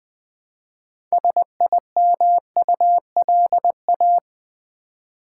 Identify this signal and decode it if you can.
Morse code, heard as a keyed tone, about 20 wpm, 700 Hz. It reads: SIMULA
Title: Build A Word - 2 Letters At A Time - 8-Letter Words - 20wpm